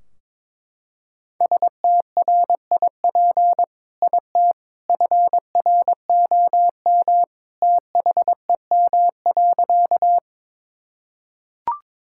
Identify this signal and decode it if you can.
Morse code, heard as a keyed tone, about 22 wpm, 700 Hz. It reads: STRIP IT FROM THEM.
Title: Sentences from Top 1600 Words 22wpm